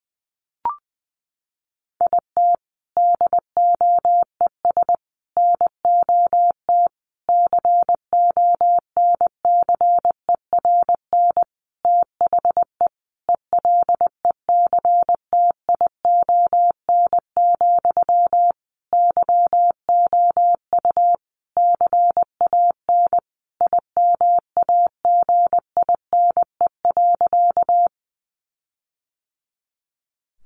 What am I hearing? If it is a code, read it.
Morse code, heard as a keyed tone, about 20 wpm, 700 Hz. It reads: IT DOES NOT CONCERN THE ELECTION, YOU CAN IMAGINE.
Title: Sentences from Top 1500 Words 20wpm